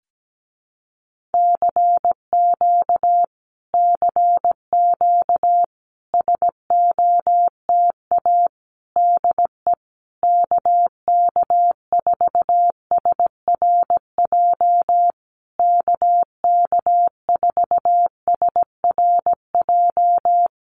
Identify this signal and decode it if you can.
Morse code, heard as a keyed tone, about 17 wpm, 700 Hz. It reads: CQ CQ SOTA DE KK4SRJ KK4SRJ
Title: Common QSO Phrases Encore 17wpm